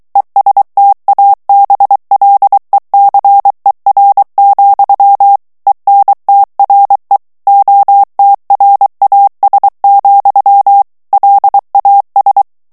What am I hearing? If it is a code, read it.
Morse code, heard as a keyed tone, about 23 wpm, 800 Hz. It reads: ESTABLECER, ENTRE OTRAS, LAS
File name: morse mp3 / ESPAGNOL TEXTE EN CLAIR 2 VITESSE 22 MOTS MINUTES